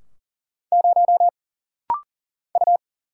Morse code, heard as a keyed tone, about 39 wpm, 700 Hz. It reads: 0 U
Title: Single Letter-Number Alt - Rapid-Fire 40wpm